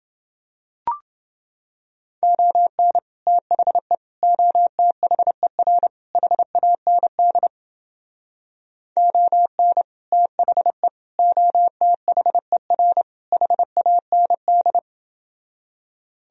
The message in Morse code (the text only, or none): ON THE OTHER HAND ON THE OTHER HAND